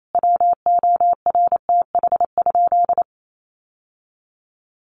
Morse code, heard as a keyed tone, about 28 wpm, 700 Hz. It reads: WORTH?
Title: Sentences from Top 800 Words 28wpm